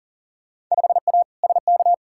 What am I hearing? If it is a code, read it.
Morse code, heard as a keyed tone, about 40 wpm, 700 Hz. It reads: 5A SK
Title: Field Day Exchanges Encore - 40wpm